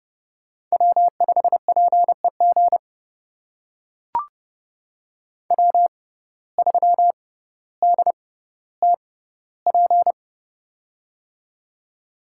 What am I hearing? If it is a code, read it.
Morse code, heard as a keyed tone, about 32 wpm, spaced as for 16 wpm, 700 Hz. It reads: W5PEG W3DTP
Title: USA Call Signs - 6x Character Spacing - 30wpm